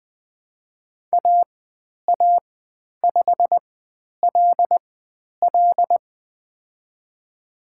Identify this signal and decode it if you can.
Morse code, heard as a keyed tone, about 20 wpm, spaced as for 10 wpm, 700 Hz. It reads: AA5LL
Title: 109 Lesson 18g - Letter m - Call Signs 20wpm